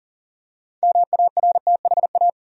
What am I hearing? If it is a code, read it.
Morse code, heard as a keyed tone, about 40 wpm, 700 Hz. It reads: MARTHA
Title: USA Common Names - 40wpm